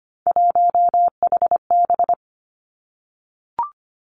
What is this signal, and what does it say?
Morse code, heard as a keyed tone, about 25 wpm, 700 Hz. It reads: 1HB
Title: Three Challenging Letter-Number Combinations 25wpm